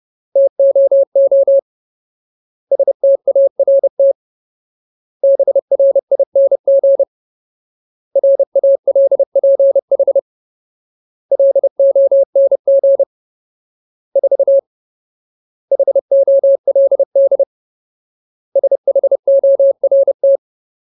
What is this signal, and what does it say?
Morse code, heard as a keyed tone, about 30 wpm, 550 Hz. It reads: TOO START BRING RALPH LONG 4 HOLD SHORT
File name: Ultimate Word List / 30WPM Ultimate